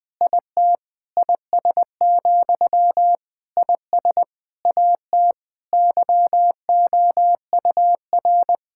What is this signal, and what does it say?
Morse code, heard as a keyed tone, about 20 wpm, 700 Hz. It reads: IT IS, IS AT YOUR
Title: Sentences from Top 1600 Words 20wpm